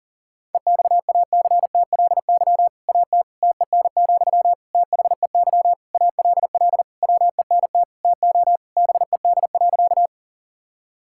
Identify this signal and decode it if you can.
Morse code, heard as a keyed tone, about 37 wpm, 700 Hz. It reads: EXACTLY AT TEN, THEY ALL WENT TO BED.
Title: Sentences from Top 1000 Words - Review Part 6 - 40wpm